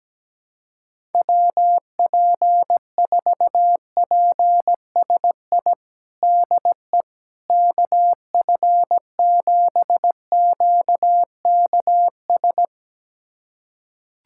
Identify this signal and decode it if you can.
Morse code, heard as a keyed tone, about 17 wpm, 700 Hz. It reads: WP4PSI DE KF7QKS
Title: USA Callsign DE Callsign - 6 Characters - 17wpm